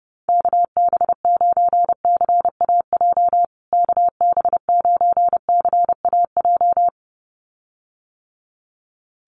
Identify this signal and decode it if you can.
Morse code, heard as a keyed tone, about 28 wpm, 700 Hz. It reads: KB9CAJ KB9CAJ
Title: USA Call Signs Repeated 2x 30wpm